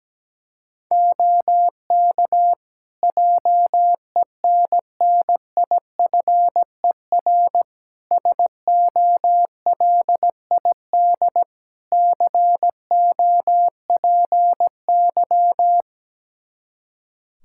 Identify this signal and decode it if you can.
Morse code, heard as a keyed tone, about 17 wpm, 700 Hz. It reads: OK JENNIFER SOLID COPY
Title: Practice QSO - Exchange 3 - More Info - 17wpm